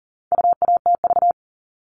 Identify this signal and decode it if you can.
Morse code, heard as a keyed tone, about 40 wpm, 700 Hz. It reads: UATV
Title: All Cut Numbers - 4-Digit 40wpm